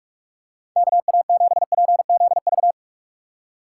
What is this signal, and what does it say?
Morse code, heard as a keyed tone, about 46 wpm, 700 Hz. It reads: KA7PZV
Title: USA Call Signs - 6 Characters - 3x Character Spacing - 45wpm